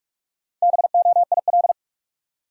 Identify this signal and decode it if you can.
Morse code, heard as a keyed tone, about 41 wpm, 700 Hz. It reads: BOIL
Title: Sets of 5 Words Encore - New Word Drill - 45wpm